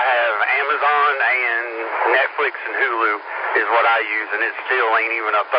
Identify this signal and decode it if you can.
Emergency Alert System (SAME) tones, heard as harsh buzzing bursts, with speech. It is heard from 1.2 to 4.4 s.